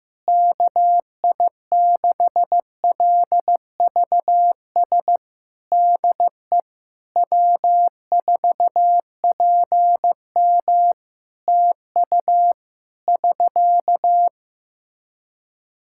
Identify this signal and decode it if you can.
Morse code, heard as a keyed tone, about 15 wpm, 700 Hz. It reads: KI6LVS DE W4PM TU <SK>
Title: Practice QSO - Exchange 4 - Ending - 15wpm